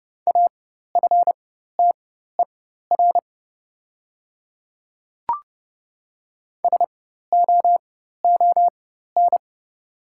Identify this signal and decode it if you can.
Morse code, heard as a keyed tone, about 30 wpm, spaced as for 14 wpm, 690 Hz. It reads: AFTER SOON
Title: 4x Character Spacing - Top 500 Words - 30wpm